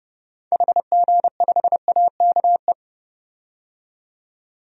Morse code, heard as a keyed tone, about 32 wpm, 700 Hz. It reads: HG5AKE
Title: DXCC-100 Call Signs - 6 Characters - 30wpm